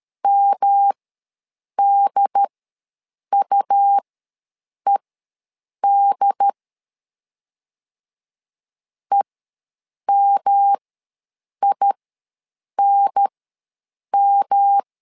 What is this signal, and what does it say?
Morse code, heard as a keyed tone, about 13 wpm, spaced as for 7 wpm, 800 Hz. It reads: MDUED EMINM
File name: LEZIONI / 3 MWRO / 3 M ws 12wpm 1745